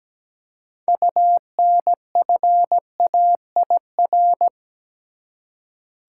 Morse code, heard as a keyed tone, about 17 wpm, 700 Hz. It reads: UNFAIR
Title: Six Letter Words - 17wpm